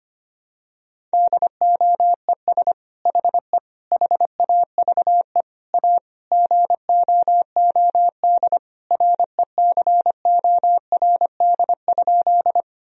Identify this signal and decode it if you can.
Morse code, heard as a keyed tone, about 25 wpm, 700 Hz. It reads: DOES HE HAVE A GOOD RECORD?
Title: Sentences from Top 1000 Words - Review Part 3 - 25wpm